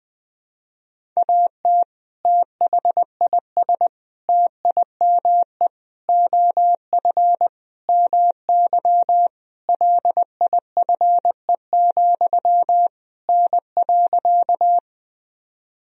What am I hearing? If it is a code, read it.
Morse code, heard as a keyed tone, about 20 wpm, 700 Hz. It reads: AT THIS TIME OF MY LIFE, N.
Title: Sentences from Top 300 Words and Abbreviations 20wpm